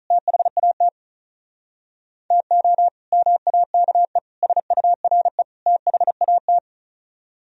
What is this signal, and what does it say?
Morse code, heard as a keyed tone, about 45 wpm, 700 Hz. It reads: THAT TO MAKE SURE THAT